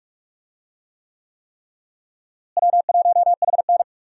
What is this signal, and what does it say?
Morse code, heard as a keyed tone, about 46 wpm, 700 Hz. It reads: W1HN
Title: USA Call Signs - 4 Characters - Repeated 3x - 45wpm